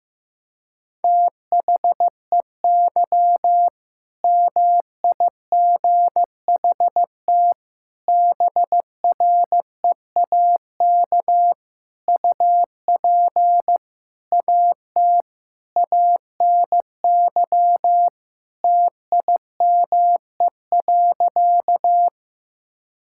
Morse code, heard as a keyed tone, about 15 wpm, 700 Hz. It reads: THEY MIGHT BREAK UP AT ANY TIME.
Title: Sentences from Top 1000 Words - Review Part 6 - 15wpm